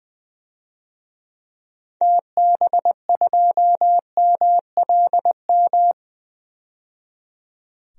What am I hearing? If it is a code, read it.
Morse code, heard as a keyed tone, about 20 wpm, 700 Hz. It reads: TB2MLM
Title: DXCC-100 Call Signs - 6 Characters - 20wpm